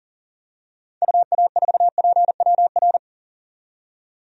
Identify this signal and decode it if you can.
Morse code, heard as a keyed tone, about 40 wpm, 700 Hz. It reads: UA4PWR
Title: DXCC-100 Call Signs - 6 Characters - 40wpm